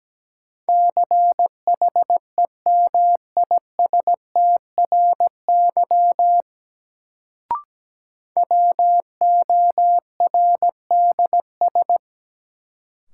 Morse code, heard as a keyed tone, about 17 wpm, 700 Hz. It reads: CHEMISTRY WORDS
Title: Sentences from Top 1000 Words - Review Part 6 - New Word Drill - 17wpm